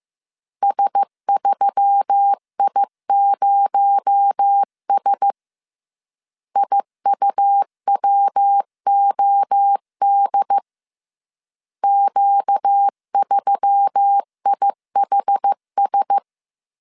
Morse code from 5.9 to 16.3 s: IUWOD Q3IHS